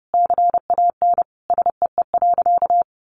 Morse code, heard as a keyed tone, about 30 wpm, 700 Hz. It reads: CAN SEE.
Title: Sentences from Top 1000 Words 30wpm